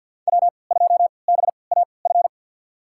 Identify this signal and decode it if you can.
Morse code, heard as a keyed tone, about 52 wpm, spaced as for 28 wpm, 700 Hz. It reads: W2BAF
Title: USA Call Signs - 3x Character Spacing - 50wpm